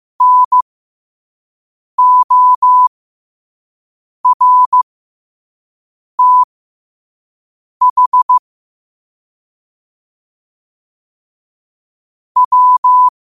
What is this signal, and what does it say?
Morse code, heard as a keyed tone, about 15 wpm, spaced as for 5 wpm, 1000 Hz. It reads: NORTH W
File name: qotdfiles / 5wpm